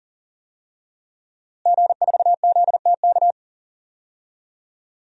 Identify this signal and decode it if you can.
Morse code, heard as a keyed tone, about 40 wpm, 700 Hz. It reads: G4ZTK
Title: DXCC-100 Call Signs - 5 Characters - Repeated 2x - 40wpm